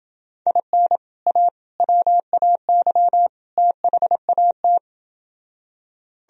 Morse code, heard as a keyed tone, about 28 wpm, 700 Hz. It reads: IN A WAY THAT